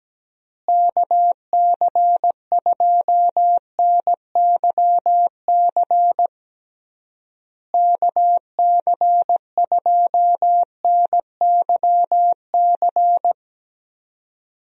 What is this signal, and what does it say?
Morse code, heard as a keyed tone, about 17 wpm, 700 Hz. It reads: KC2NYC KC2NYC
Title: USA Call Signs - 6 Characters - Repeated 3x - 17wpm